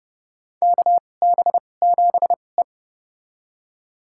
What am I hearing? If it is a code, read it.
Morse code, heard as a keyed tone, about 32 wpm, spaced as for 22 wpm, 700 Hz. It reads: KB7E
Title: USA Call Signs - 2x Character Spacing - 30wpm